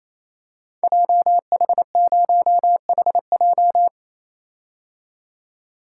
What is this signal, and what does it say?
Morse code, heard as a keyed tone, about 27 wpm, 700 Hz. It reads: JH0HJ
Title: Code Groups - 5 Letter:Numbers - 28wpm